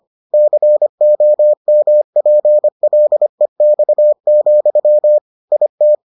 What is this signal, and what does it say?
Morse code, heard as a keyed tone, about 25 wpm, 600 Hz. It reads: COMPLEX, IT